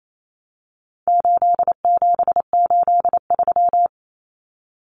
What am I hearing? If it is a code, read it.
Morse code, heard as a keyed tone, about 28 wpm, 700 Hz. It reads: 8783
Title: Common Cut Numbers - 4-Digit 28wpm